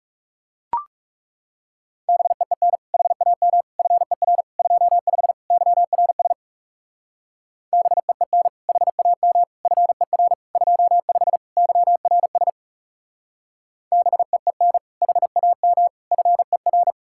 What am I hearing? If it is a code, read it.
Morse code, heard as a keyed tone, about 36 wpm, 700 Hz. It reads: BEEN HAM FER 25 YRS BEEN HAM FER 25 YRS BEEN HAM FER
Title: Speed-Racing Common QSO Phrases Encore - 30wpm